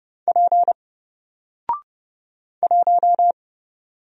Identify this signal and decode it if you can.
Morse code, heard as a keyed tone, about 30 wpm, 700 Hz. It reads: P 1
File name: Single-Letter-Number-30wpm